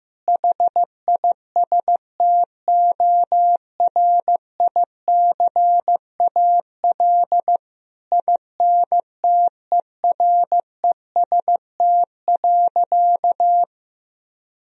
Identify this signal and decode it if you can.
Morse code, heard as a keyed tone, about 15 wpm, 700 Hz. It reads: HISTORICAL INTEREST.
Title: Sentences from Top 1000 Words - Review Part 4 - 15wpm